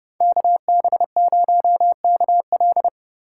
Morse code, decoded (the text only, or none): KB0KL